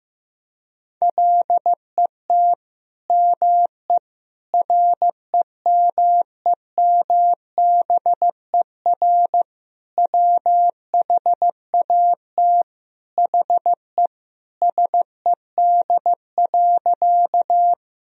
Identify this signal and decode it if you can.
Morse code, heard as a keyed tone, about 15 wpm, 700 Hz. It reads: LET ME REMEMBER WHAT HE SED.
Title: Sentences from Top 400 Words and Abbreviations 15wpm